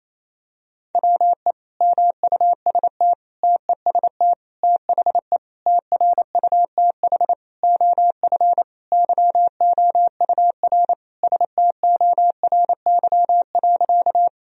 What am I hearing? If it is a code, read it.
Morse code, heard as a keyed tone, about 28 wpm, 700 Hz. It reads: WE MUST TEST THE TRUTH OF YOUR STORY.
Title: Sentences from Top 1000 Words - Review Part 3 - 28wpm